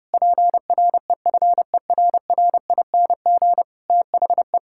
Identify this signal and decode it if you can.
Morse code, heard as a keyed tone, about 30 wpm, 700 Hz. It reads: PREFERRING THE
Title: Pangram Challenge v2 - 30wpm